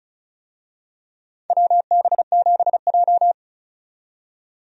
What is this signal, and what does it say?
Morse code, heard as a keyed tone, about 34 wpm, 700 Hz. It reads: WB7J
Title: USA Call Signs - 4 Characters - 35wpm